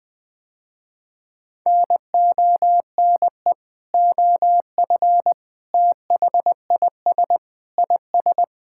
Morse code, heard as a keyed tone, about 21 wpm, 700 Hz. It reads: NONE OF THIS IS
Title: Sentences with 7-Word Phrases - Spoken Beginning - 20wpm